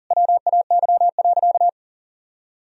Morse code, heard as a keyed tone, about 40 wpm, 700 Hz. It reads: WAY.
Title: Sentences from Top 300 Words and Abbreviations 40wpm